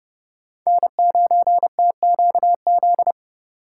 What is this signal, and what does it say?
Morse code, heard as a keyed tone, about 31 wpm, 700 Hz. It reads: N9TQZ
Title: USA Call Signs - 5 Characters - 3x Character Spacing - 30wpm